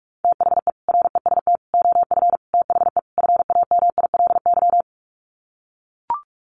Morse code, heard as a keyed tone, about 44 wpm, 700 Hz. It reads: THE REST OF THE FAMILY
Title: Sets of 5 Words Encore2 45wpm